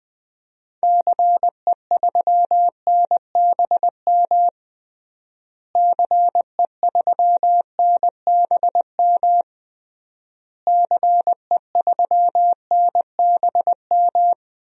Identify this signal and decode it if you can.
Morse code, heard as a keyed tone, about 20 wpm, 700 Hz. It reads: CE3NBM CE3NBM CE3NBM
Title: DXCC-100 Call Signs - 6 Characters - Repeated 3x - 20wpm